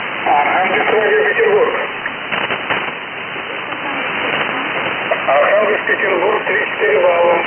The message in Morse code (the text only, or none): none